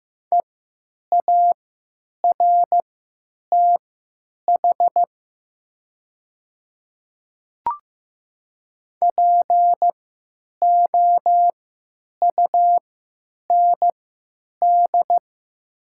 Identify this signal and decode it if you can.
Morse code, heard as a keyed tone, about 15 wpm, spaced as for 9 wpm, 700 Hz. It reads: EARTH POUND
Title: 3x Character Spacing - Top 500 Words - 15wpm